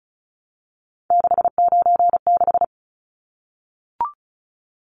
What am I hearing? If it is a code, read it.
Morse code, heard as a keyed tone, about 34 wpm, 700 Hz. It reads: 696
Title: Numbers - 3-Digit 35wpm